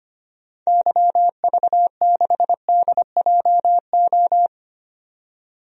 Morse code, heard as a keyed tone, about 25 wpm, 700 Hz. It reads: YV6DJO
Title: DXCC-100 Call Signs - 6 Characters - 25wpm